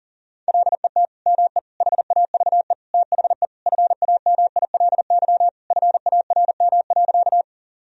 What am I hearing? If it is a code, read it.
Morse code, heard as a keyed tone, about 37 wpm, 700 Hz. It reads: LET ME HAVE THE FAMILY FARM.
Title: Sentences from Top 1000 Words - Review Part 2 - 40wpm